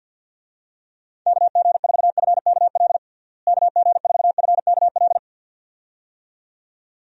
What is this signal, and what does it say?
Morse code, heard as a keyed tone, about 46 wpm, 700 Hz. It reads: KG4FKL KG4FKL
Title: USA Call Signs - 6 Characters - Repeated 2x - 50wpm